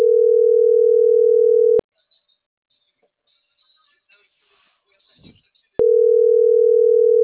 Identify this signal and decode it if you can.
Ringback tone (North America), 440 + 480 Hz.
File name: CENTRAL MAYAB / out dulce.silva